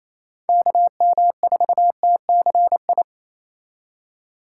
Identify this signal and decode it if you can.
Morse code, heard as a keyed tone, about 36 wpm, 700 Hz. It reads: KM4TCI